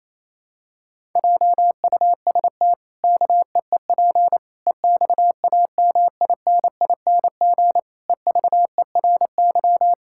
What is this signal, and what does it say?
Morse code, heard as a keyed tone, about 28 wpm, 700 Hz. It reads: JUST KEEP EXAMINING EVERY
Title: Pangram Challenge v2 - 28wpm